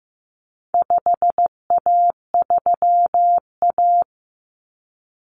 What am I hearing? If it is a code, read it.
Morse code, heard as a keyed tone, about 15 wpm, 700 Hz. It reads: 5A3A
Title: Common Cut Numbers - 4-Digit 15wpm